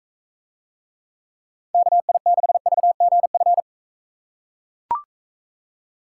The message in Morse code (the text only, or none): KI6VGF